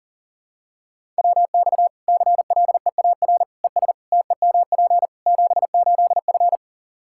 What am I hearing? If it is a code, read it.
Morse code, heard as a keyed tone, about 37 wpm, 700 Hz. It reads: WX CLEAR ES TEMP 78F
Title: K1USN SST QSO Practice 40wpm